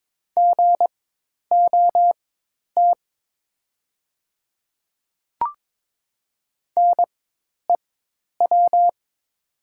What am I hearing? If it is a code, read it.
Morse code, heard as a keyed tone, about 22 wpm, spaced as for 10 wpm, 690 Hz. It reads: GOT NEW
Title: 4x Character Spacing - Top 300 Words - 22wpm